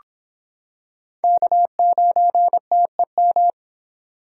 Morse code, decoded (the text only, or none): K9TEM